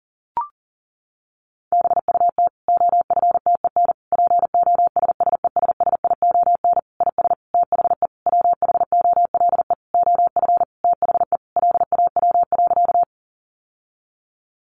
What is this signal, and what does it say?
Morse code, heard as a keyed tone, about 40 wpm, 700 Hz. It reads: BUT OFTEN POSSESSION IS THE WHOLE OF THE LAW.
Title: Sentences from Top 1400 Words 40wpm